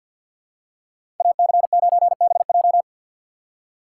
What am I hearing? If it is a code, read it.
Morse code, heard as a keyed tone, about 52 wpm, 700 Hz. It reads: AC9BJ
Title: USA Call Signs - 5 Characters - 3x Character Spacing - 50wpm